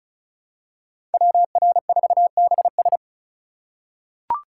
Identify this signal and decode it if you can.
Morse code, heard as a keyed tone, about 34 wpm, 700 Hz. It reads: WR4BS
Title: USA Call Signs - 5 Characters - 3x Character Spacing - 35wpm